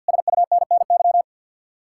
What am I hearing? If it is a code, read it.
Morse code, heard as a keyed tone, about 67 wpm, spaced as for 58 wpm, 700 Hz. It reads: SUNNY
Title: Speed-Racing Common QSO Phrases Encore - 50wpm